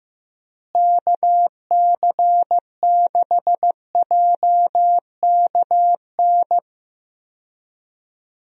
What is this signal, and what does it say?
Morse code, heard as a keyed tone, about 15 wpm, 700 Hz. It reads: KC6JKN
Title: USA Call Signs - 6 Characters - Repeated 2x - 15wpm